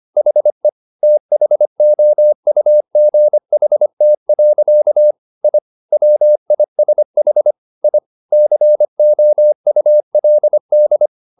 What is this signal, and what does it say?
Morse code, heard as a keyed tone, about 25 wpm, 600 Hz. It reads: HE THOUGHT. I WISH I COULD
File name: the old man and the sea part 1 25wpm